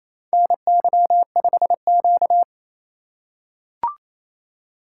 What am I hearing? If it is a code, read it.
Morse code, heard as a keyed tone, about 28 wpm, 700 Hz. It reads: NY5Q
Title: USA Call Signs - 4 Characters - 3x Character Spacing - 28wpm